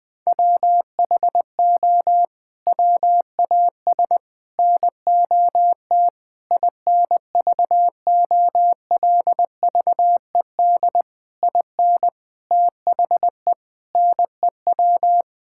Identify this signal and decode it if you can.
Morse code, heard as a keyed tone, about 20 wpm, 700 Hz. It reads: WHO WAS NOT INVOLVED IN THE NEW
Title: Sets of 8 Words - 20wpm